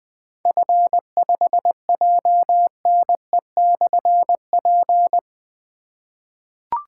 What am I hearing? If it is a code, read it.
Morse code, heard as a keyed tone, about 21 wpm, 700 Hz. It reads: F5JNE/P
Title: Difficult Call Signs 20wpm